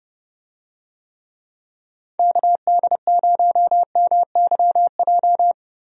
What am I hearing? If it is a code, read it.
Morse code, heard as a keyed tone, about 30 wpm, 700 Hz. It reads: KD0MYJ
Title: USA Call Signs - 6 Characters - Repeated 2x - 30wpm